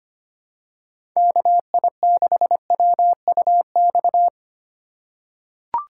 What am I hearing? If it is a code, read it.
Morse code, heard as a keyed tone, about 25 wpm, 700 Hz. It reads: KI6WUX
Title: USA Call Signs - 3x Character Spacing - 25wpm